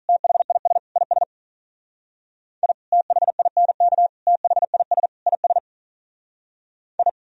Morse code, heard as a keyed tone, about 44 wpm, 700 Hz. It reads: THIS IS I THINK THIS IS I